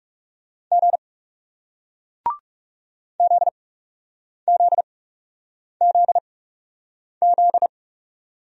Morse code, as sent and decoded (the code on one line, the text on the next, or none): --. / --.. --.. --.. --..
G ZZZZ